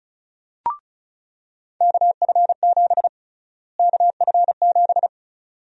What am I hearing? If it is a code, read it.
Morse code, heard as a keyed tone, about 34 wpm, 700 Hz. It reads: KF7 KF7
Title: USA Call Sign Prefixes - Repeated 2x - 35wpm